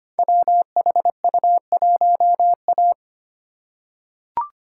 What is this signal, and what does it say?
Morse code, heard as a keyed tone, about 25 wpm, 700 Hz. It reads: WHU1A
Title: Code Groups - 5 Letter:Numbers - 25wpm